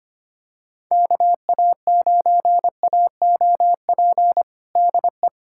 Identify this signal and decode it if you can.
Morse code, heard as a keyed tone, about 25 wpm, 700 Hz. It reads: KA9AOP DE
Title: USA Callsign DE Callsign - 6 Characters - 25wpm